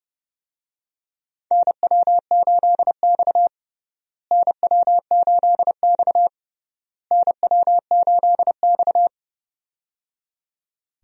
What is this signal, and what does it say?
Morse code, heard as a keyed tone, about 30 wpm, 700 Hz. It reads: NW8X NW8X NW8X
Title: USA Call Signs - 4 Characters - Repeated 3x - 30wpm